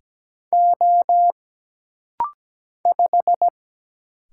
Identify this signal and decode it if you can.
Morse code, heard as a keyed tone, about 17 wpm, 700 Hz. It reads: O 5
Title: Single Letter-Number Alt - Rapid-Fire 17wpm